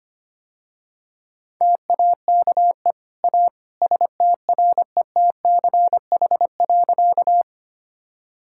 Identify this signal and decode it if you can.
Morse code, heard as a keyed tone, about 25 wpm, 700 Hz. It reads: TAKE A STRETCH.
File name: Easy Sentences with Hints - 25wpm